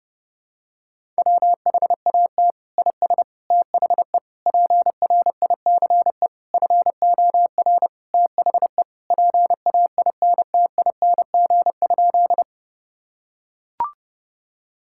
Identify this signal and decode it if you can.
Morse code, heard as a keyed tone, about 30 wpm, 700 Hz. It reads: WHAT IS THE PRICE FOR THE PAINTING?
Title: Sentences from Top 1000 Words - Review Part 6 - 30wpm